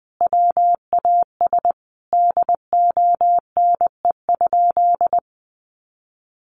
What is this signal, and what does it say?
Morse code, heard as a keyed tone, about 20 wpm, 700 Hz. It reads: WAS DONE?
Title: Sentences from Top 1100 Words 20wpm